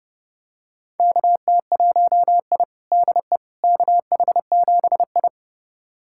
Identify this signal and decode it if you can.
Morse code, heard as a keyed tone, about 30 wpm, 700 Hz. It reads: KT1I DE KH7I
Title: USA Callsign DE Callsign - 4 Characters - 30wpm